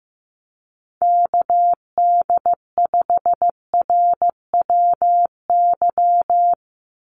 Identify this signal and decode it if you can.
Morse code, heard as a keyed tone, about 15 wpm, 700 Hz. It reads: KD5RWY
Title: USA Call Signs 15wpm